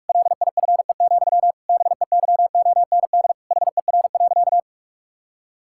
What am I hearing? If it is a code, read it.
Morse code, heard as a keyed tone, about 46 wpm, 700 Hz. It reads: LIFE, BEYOND HER.
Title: Sentences from Top 1000 Words - Review Part 6 - 45wpm